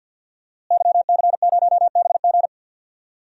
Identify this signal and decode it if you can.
Morse code, heard as a keyed tone, about 47 wpm, 700 Hz. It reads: YC0BG
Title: DXCC-100 Call Signs - 5 Characters - Repeated 2x - 50wpm